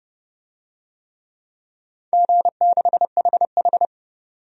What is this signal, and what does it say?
Morse code, heard as a keyed tone, about 30 wpm, 700 Hz. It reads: G6HH
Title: DXCC-100 Call Signs - 4 Characters - 30wpm